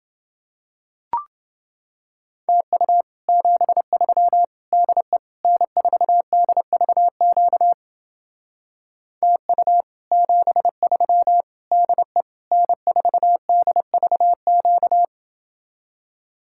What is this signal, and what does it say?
Morse code, heard as a keyed tone, about 29 wpm, 700 Hz. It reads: TU 73 DE N4DVQ TU 73 DE N4DVQ